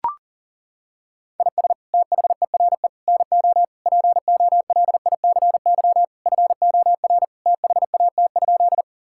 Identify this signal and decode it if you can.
Morse code, heard as a keyed tone, about 37 wpm, 700 Hz. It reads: IS THERE NO POLICY FOR THAT?
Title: Sentences from Top 1000 Words - Review Part 5 - 40wpm